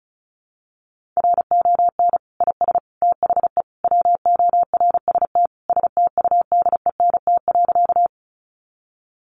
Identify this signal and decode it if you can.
Morse code, heard as a keyed tone, about 34 wpm, 700 Hz. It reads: RON IS THE WORST STUDENT.
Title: Additional Easy Phrases 35wpm